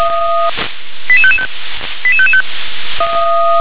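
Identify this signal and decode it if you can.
Five-tone selective call (ZVEI), heard as quick runs of five tones, tones of about 70 ms. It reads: 88555 88555
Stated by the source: ZVEI five-tone call received by SDR.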